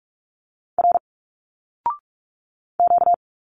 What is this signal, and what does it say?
Morse code, heard as a keyed tone, about 46 wpm, 700 Hz. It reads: R Q
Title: Single Letters 45wpm